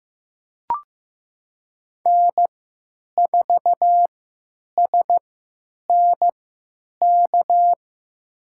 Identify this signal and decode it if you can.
Morse code, heard as a keyed tone, about 16 wpm, spaced as for 9 wpm, 700 Hz. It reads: N4SNK
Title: USA Call Signs - 3x Character Spacing - 15wpm